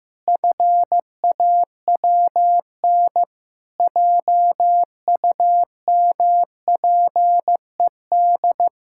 Morse code, heard as a keyed tone, about 15 wpm, 700 Hz. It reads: FAWN JUMPED
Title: Pangram Challenge v2 - 15wpm